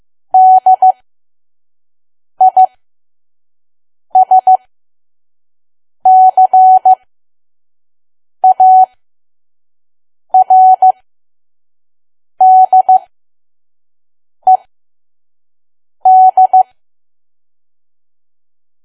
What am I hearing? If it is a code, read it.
Morse code, heard as a keyed tone, about 15 wpm, spaced as for 5 wpm, 750 Hz. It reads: DISCARDED
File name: Archive / 5 WPM files / PM